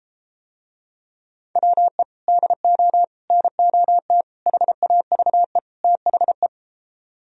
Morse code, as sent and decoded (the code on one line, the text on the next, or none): .-- . / -.. --- / -. --- - / .... .- ...- . / - .... .
WE DO NOT HAVE THE